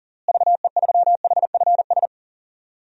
Morse code, heard as a keyed tone, about 37 wpm, 700 Hz. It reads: VE3HFS
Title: DXCC-100 Call Signs - 6 Characters - Repeated 3x - 40wpm